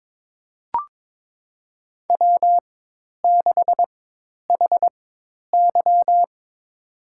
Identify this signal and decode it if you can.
Morse code, heard as a keyed tone, about 22 wpm, spaced as for 10 wpm, 700 Hz. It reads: W6HY
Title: USA Call Signs - 4 Characters - 4x Character Spacing - 22wpm